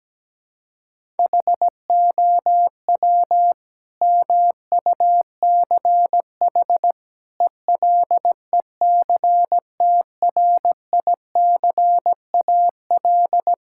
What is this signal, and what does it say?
Morse code, heard as a keyed tone, about 17 wpm, 700 Hz. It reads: HOW MUCH ELECTRICAL
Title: Sentences from Top 1000 Words - Review Part 6 - 17wpm